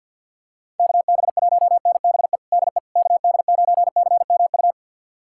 Morse code, heard as a keyed tone, about 46 wpm, 700 Hz. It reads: KB1NBE DE KD9CMU
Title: USA Callsign DE Callsign - 6 Characters - 50wpm